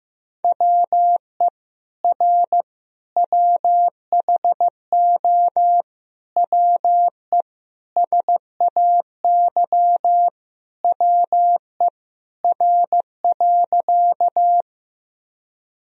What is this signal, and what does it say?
Morse code, heard as a keyed tone, about 16 wpm, 700 Hz. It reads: WE R WHO WE SAY WE R.
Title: Sentences from Top 200 Words and Abbreviations 15wpm